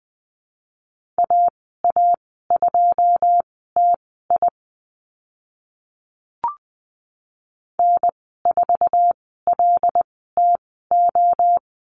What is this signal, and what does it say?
Morse code, heard as a keyed tone, about 20 wpm, spaced as for 14 wpm, 700 Hz. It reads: AA2TI N4LTO
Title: 082 Lesson 14g - Number 2 - Call Signs 20wpm